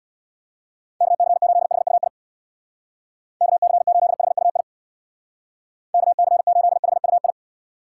Morse code, heard as a keyed tone, about 65 wpm, 700 Hz. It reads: KC8HUI KC8HUI KC8HUI